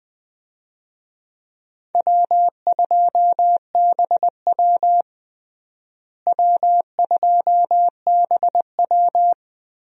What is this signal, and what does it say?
Morse code, heard as a keyed tone, about 21 wpm, 700 Hz. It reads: W2BW W2BW
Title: USA Call Signs - 4 Characters - Repeated 3x - 20wpm